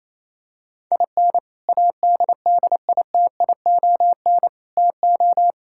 Morse code, heard as a keyed tone, about 28 wpm, 700 Hz. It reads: IN ADDITION TO
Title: Sentences from Top 1000 Words - Review Part 6 - 28wpm